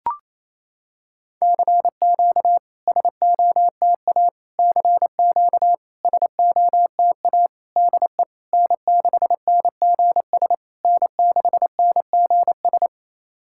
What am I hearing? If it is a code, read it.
Morse code, heard as a keyed tone, about 28 wpm, 700 Hz. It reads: CQ SOTA CQ SOTA DE N6NGS N6NGS
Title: SOTA QSO Practice 28wpm